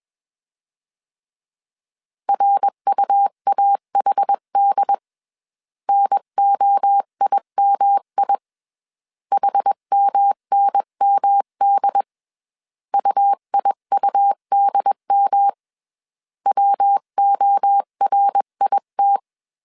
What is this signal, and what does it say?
Morse code, heard as a keyed tone, about 21 wpm, 800 Hz. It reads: RUAHD NOIMI HMNMD UIUDM WORIT
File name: CW / DET / 1 3 1500C S B 20wpm 1754